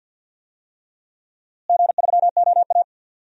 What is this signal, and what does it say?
Morse code, heard as a keyed tone, about 51 wpm, 700 Hz. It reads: G3OA
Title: DXCC-100 Call Signs - 4 Characters - Repeated 3x - 50wpm